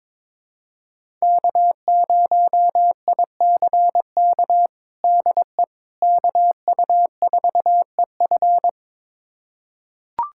Morse code, heard as a keyed tone, about 21 wpm, 700 Hz. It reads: K0ICK DE KU4EF
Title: USA Callsign DE Callsign - 5 Characters - 22wpm